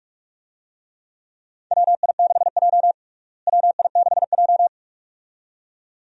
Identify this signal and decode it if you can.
Morse code, heard as a keyed tone, about 44 wpm, 700 Hz. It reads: WI6J WI6J
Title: USA Call Signs - 4 Characters - Repeated 2x - 45wpm